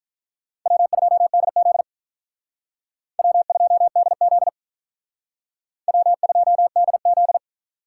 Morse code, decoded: W2DZ W2DZ W2DZ

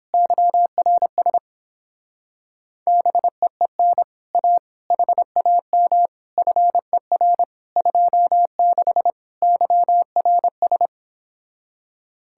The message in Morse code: YRS BEEN A HAM FER 26 YRS

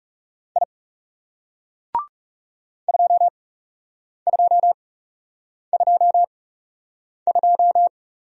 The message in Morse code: I 2222

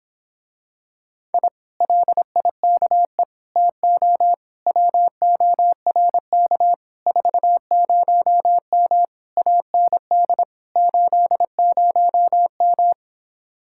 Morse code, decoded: I LIKE TO WORK 40M AND 80M

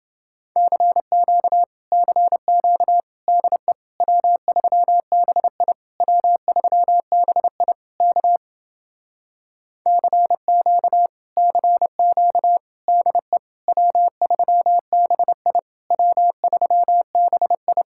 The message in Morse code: CQ CQ DE W3BI W3BI K CQ CQ DE W3BI W3BI

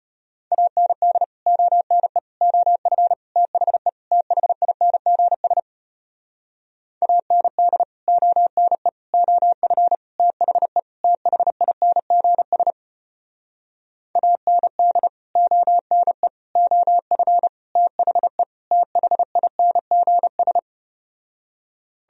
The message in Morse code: AND ONE OF THE THINGS AND ONE OF THE THINGS AND ONE OF THE THINGS

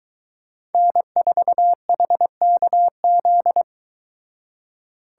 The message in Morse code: N4HKZ

